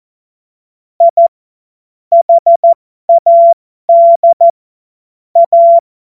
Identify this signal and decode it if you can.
Morse code, heard as a keyed tone, about 14 wpm, spaced as for 12 wpm, 680 Hz. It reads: I HAD A
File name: Morsekurs / Witze